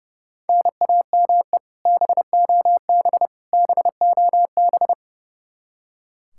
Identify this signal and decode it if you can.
Morse code, heard as a keyed tone, about 30 wpm, 700 Hz. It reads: NAME BOB BOB
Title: K1USN SST QSO Practice 30wpm